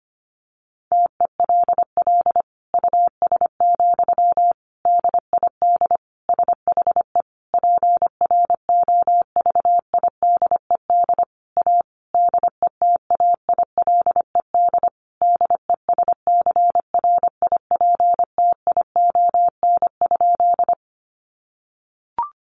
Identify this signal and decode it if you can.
Morse code, heard as a keyed tone, about 25 wpm, 700 Hz. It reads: TELL US, DID SHE PROVIDED A DETAILED DESCRIPTION?
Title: Sentences from Top 1500 Words 25wpm